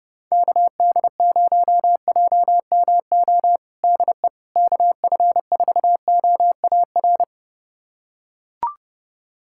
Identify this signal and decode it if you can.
Morse code, heard as a keyed tone, about 32 wpm, 700 Hz. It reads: KD0JMO DE KF4OAR
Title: USA Callsign DE Callsign - 6 Characters - 30wpm